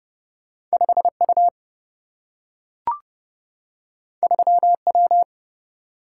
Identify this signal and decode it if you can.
Morse code, heard as a keyed tone, about 30 wpm, 700 Hz. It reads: 5 U 3 W
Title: Single-Letter Single-Number 30wpm